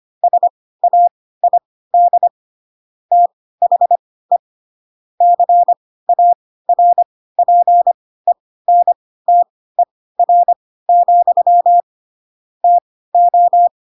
Morse code, heard as a keyed tone, about 25 wpm, spaced as for 16 wpm, 700 Hz. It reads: SAID THE CARPENTER, TO
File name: Pinocchio / Pinocchio ch2 10 25 ik1hge